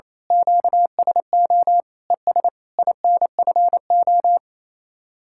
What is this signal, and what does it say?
Morse code, heard as a keyed tone, about 26 wpm, 700 Hz. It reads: QSO ES INFO
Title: Speed-Racing Common QSO Phrases Encore - 28wpm